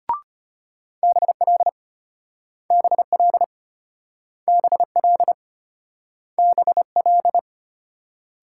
Morse code, heard as a keyed tone, about 34 wpm, 700 Hz. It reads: BL BL BL BL